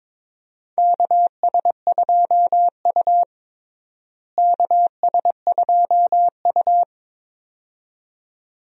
Morse code, heard as a keyed tone, about 22 wpm, 700 Hz. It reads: KS2U KS2U
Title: USA Call Signs - 4 Characters - Repeated 2x - 22wpm